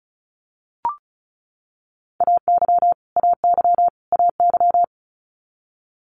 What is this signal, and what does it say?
Morse code, heard as a keyed tone, about 36 wpm, 700 Hz. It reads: AY AY AY
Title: Multi-letter Phonemes Repeated 3x 35wpm